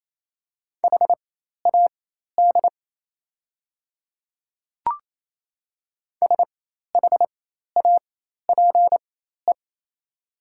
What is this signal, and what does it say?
Morse code, heard as a keyed tone, about 28 wpm, spaced as for 13 wpm, 700 Hz. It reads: HAD SHAPE
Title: 4x Character Spacing - Top 500 Words - 28wpm